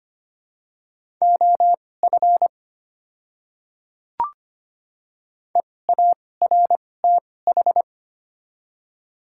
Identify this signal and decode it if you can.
Morse code, heard as a keyed tone, about 25 wpm, spaced as for 18 wpm, 700 Hz. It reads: OF EARTH
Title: 2x Character Spacing - Top 500 Words - 25wpm